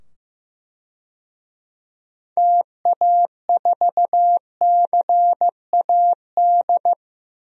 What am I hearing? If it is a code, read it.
Morse code, heard as a keyed tone, about 15 wpm, 700 Hz. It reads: TA4CAD
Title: DXCC-100 Call Signs - 6 Characters - 15wpm